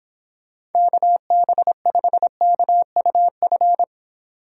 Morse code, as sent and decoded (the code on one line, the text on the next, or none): -.- -... ..... -.- ..- ..-.
KB5KUF